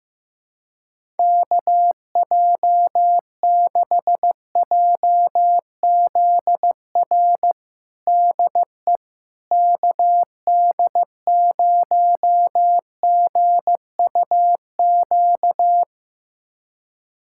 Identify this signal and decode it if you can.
Morse code, heard as a keyed tone, about 15 wpm, 700 Hz. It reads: KJ6JZR DE KD0GUQ
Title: USA Callsign DE Callsign 15wpm